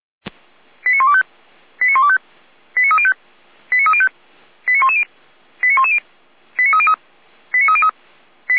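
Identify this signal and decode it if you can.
Five-tone selective call (ZVEI), heard as quick runs of five tones, tones of about 70 ms. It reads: 89125 89125 89305 89305 89110 89110 89303 89303
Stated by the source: ZVEI five-tone call received by SDR, fire brigade Feuerwehr Orlamünde.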